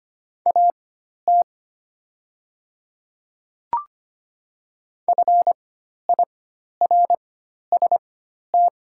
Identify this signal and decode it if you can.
Morse code, heard as a keyed tone, about 25 wpm, spaced as for 12 wpm, 700 Hz. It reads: AT FIRST
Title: 4x Character Spacing - Top 100 Words - 25wpm